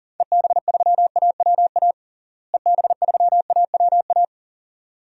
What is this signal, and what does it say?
Morse code, heard as a keyed tone, about 38 wpm, 700 Hz. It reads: EB3AWA EB3AWA
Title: DXCC-100 Call Signs - 6 Characters - Repeated 2x - 40wpm